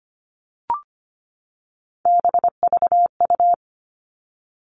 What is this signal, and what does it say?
Morse code, heard as a keyed tone, about 25 wpm, 700 Hz. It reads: BVU
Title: All Cut Numbers - 3-Digit 25wpm